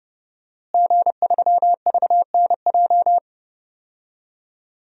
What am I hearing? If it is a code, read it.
Morse code, heard as a keyed tone, about 30 wpm, 700 Hz. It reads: G3VNJ
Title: DXCC-100 Call Signs - 5 Characters - Repeated 3x - 30wpm